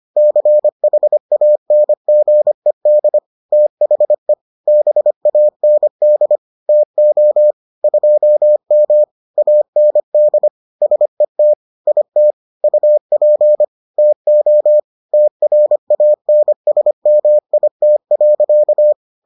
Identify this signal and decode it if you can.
Morse code, heard as a keyed tone, about 25 wpm, 600 Hz. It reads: CHANGED THE BAND TO 2M AND SET IT UP TO TRANSMIT.